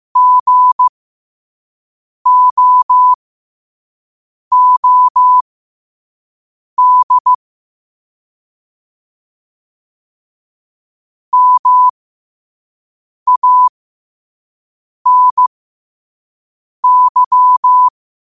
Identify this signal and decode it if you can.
Morse code, heard as a keyed tone, about 15 wpm, spaced as for 5 wpm, 1000 Hz. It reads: GOOD MANY
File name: qotdfiles / 5wpm